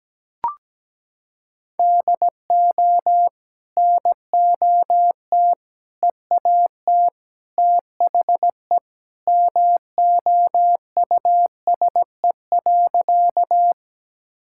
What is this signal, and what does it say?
Morse code, heard as a keyed tone, about 17 wpm, 700 Hz. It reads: DO NOT EAT THE MOUSE.
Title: Sentences from Top 1000 Words - Review Part 5 - 17wpm